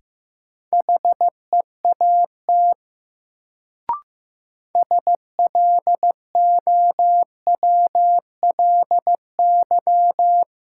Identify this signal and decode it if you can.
Morse code, heard as a keyed tone, about 15 wpm, 700 Hz. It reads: HEAT SLOWLY
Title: Sentences from Top 1100 Words - New Word Drill Alt - 15wpm